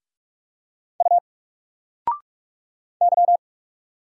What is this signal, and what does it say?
Morse code, heard as a keyed tone, about 46 wpm, 700 Hz. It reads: U Y
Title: Single Letter-Number 45wpm